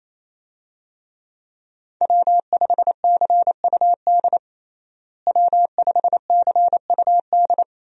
Morse code, heard as a keyed tone, about 28 wpm, 700 Hz. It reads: W5CUD W5CUD
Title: USA Call Signs - 5 Characters - Repeated 2x - 28wpm